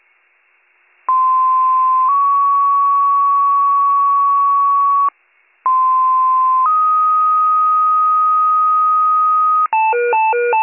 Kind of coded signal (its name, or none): two-tone page